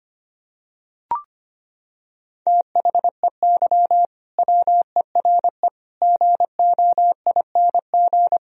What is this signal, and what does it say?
Morse code, heard as a keyed tone, about 25 wpm, 700 Hz. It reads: THEY WERE GOING
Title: Sentences from Top 1000 Words - Review Part 4 - 25wpm